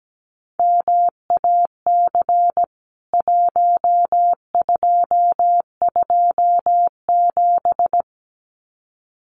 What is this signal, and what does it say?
Morse code, heard as a keyed tone, about 17 wpm, 700 Hz. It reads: MAC 1227
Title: CWT Exchange 17wpm